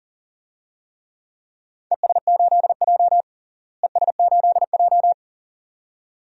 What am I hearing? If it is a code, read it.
Morse code, heard as a keyed tone, about 40 wpm, 700 Hz. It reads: ES8J ES8J
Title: DXCC-100 Call Signs - 4 Characters - Repeated 2x - 40wpm